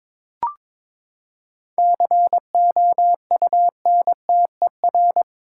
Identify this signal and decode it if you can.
Morse code, heard as a keyed tone, about 22 wpm, 700 Hz. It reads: COUNTER
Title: Seven Letter Words - 22wpm